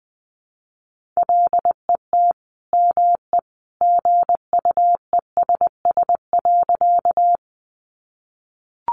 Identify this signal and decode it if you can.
Morse code, heard as a keyed tone, about 20 wpm, 700 Hz. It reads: LET ME GUESS.
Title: Sentences from Top 1500 Words 20wpm